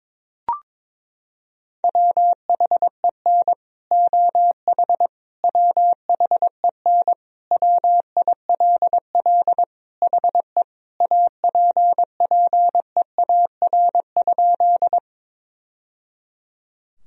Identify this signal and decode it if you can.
Morse code, heard as a keyed tone, about 22 wpm, 700 Hz. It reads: WHEN OH WHEN WILL HE APPEAR?
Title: Sentences from Top 1000 Words - Review Part 2 - 22wpm